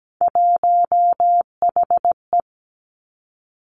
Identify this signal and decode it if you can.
Morse code, heard as a keyed tone, about 17 wpm, 700 Hz. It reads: 1HE
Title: Three Challenging Letter-Number Combinations 17wpm